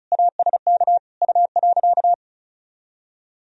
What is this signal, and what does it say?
Morse code, heard as a keyed tone, about 34 wpm, 700 Hz. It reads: ASK U.
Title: Sentences from Top 400 Words and Abbreviations 35wpm